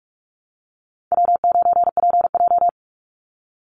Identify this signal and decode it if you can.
Morse code, heard as a keyed tone, about 44 wpm, 700 Hz. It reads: W9PJ
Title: Easy Call Signs Repeated 2x - 45wpm